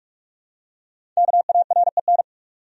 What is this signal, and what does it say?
Morse code, heard as a keyed tone, about 46 wpm, 700 Hz. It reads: KAREN
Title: USA Common Names - Max 5 Letters - 45wpm